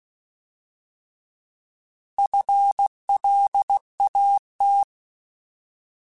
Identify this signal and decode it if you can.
Morse code, heard as a keyed tone, about 16 wpm, 800 Hz. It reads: FLAT
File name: Alphabetical / Alphabetical mp3s / Fs @15